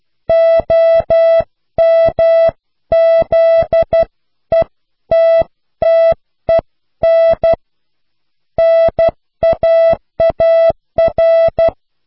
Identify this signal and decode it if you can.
Morse code, heard as a keyed tone, about 12 wpm, 660 Hz. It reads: OMZETTEN NAAR